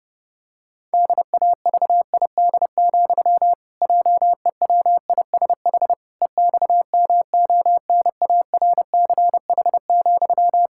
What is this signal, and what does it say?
Morse code, heard as a keyed tone, about 32 wpm, 700 Hz. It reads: DAVID, JEWISH EXMONARCH,
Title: Pangram Challenge v2 - 30wpm